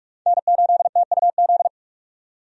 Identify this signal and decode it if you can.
Morse code, heard as a keyed tone, about 46 wpm, 700 Hz. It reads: N8TUZ
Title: USA Call Signs - 5 Characters - 45wpm